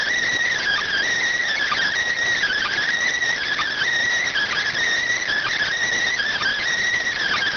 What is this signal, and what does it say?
Emergency Alert System (SAME) tones (header bursts), heard as harsh buzzing bursts. It is heard from 0.5 to 6.6 s.